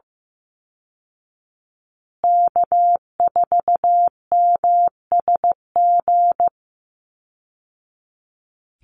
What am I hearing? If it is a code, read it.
Morse code, heard as a keyed tone, about 15 wpm, 700 Hz. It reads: K4MSG
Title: Call Signs Repeated 2x - 15wpm